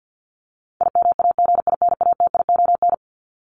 Morse code, heard as a keyed tone, about 50 wpm, 700 Hz. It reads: IMAGINATION
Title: Top 1001-1250 Words 50wpm